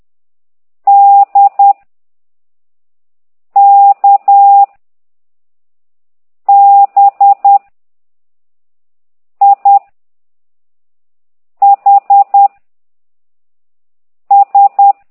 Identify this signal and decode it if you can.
Morse code, heard as a keyed tone, about 10 wpm, 800 Hz. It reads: D K B I H S
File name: ct2idw / exerc9